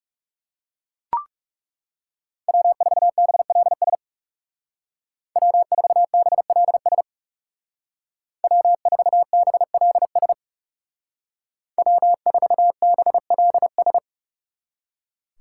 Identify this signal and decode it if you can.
Morse code, heard as a keyed tone, about 36 wpm, 700 Hz. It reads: W4BLS W4BLS W4BLS W4BLS
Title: Speed Racing - USA 1x3 Call Signs 30wpm